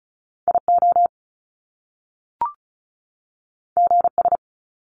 Morse code, heard as a keyed tone, about 36 wpm, 700 Hz. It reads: IO GS
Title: Top Two Letter Combinations in Words 35wpm